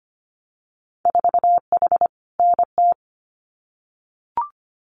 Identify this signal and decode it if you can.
Morse code, heard as a keyed tone, about 25 wpm, 690 Hz. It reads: 4H NT
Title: Winter Field Day Exchanges Encore 25wpm